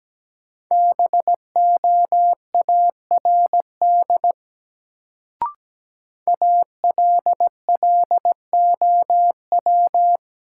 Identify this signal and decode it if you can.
Morse code, heard as a keyed tone, about 17 wpm, 700 Hz. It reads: BOARD ALLOW
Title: Sentences from Top 700 Words - New Word Drill - 17wpm